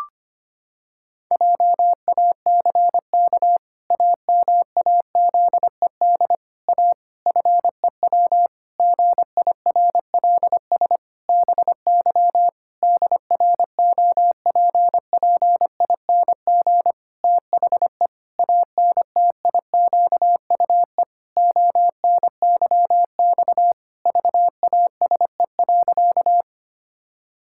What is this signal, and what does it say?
Morse code, heard as a keyed tone, about 25 wpm, 700 Hz. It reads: JACK AMAZED A FEW GIRLS BY DROPPING THE ANTIQUE ONYX VASE.
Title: Pangram Challenge v2 - 25wpm